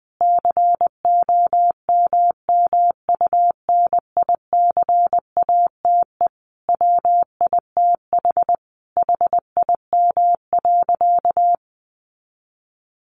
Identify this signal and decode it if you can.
Morse code, heard as a keyed tone, about 20 wpm, 700 Hz. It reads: COMMUNICATE WITH HIM.
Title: Sentences from Top 1300 Words 20wpm